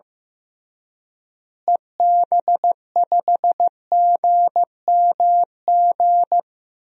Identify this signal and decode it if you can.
Morse code, heard as a keyed tone, about 15 wpm, 700 Hz. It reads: EB5GMG
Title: DXCC-100 Call Signs - 6 Characters - Repeated 3x - 15wpm